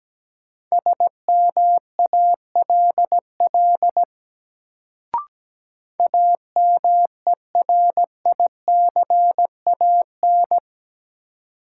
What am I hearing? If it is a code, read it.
Morse code, heard as a keyed tone, about 17 wpm, 700 Hz. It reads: SMALL AMERICAN
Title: Sets of 2 Words - New Word Drill - 17wpm